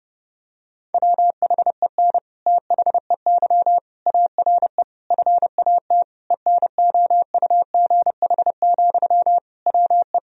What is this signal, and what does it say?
Morse code, heard as a keyed tone, about 32 wpm, 700 Hz. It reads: WHEN THEY ARE FAT ENOUGH, WE
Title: Sentences from Top 1000 Words - Review Part 4 - 30wpm